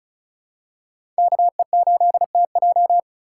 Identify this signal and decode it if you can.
Morse code, heard as a keyed tone, about 34 wpm, 700 Hz. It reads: KE8TJ
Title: USA Call Signs - 5 Characters - 4x Character Spacing - 35wpm